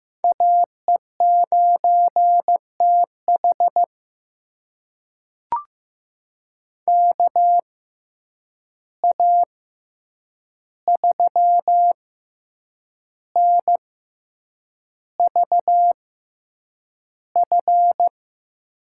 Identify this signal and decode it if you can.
Morse code, heard as a keyed tone, about 15 wpm, spaced as for 7 wpm, 700 Hz. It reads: AE9TH KA3NVF
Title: USA Call Signs - 6x Character Spacing - 15wpm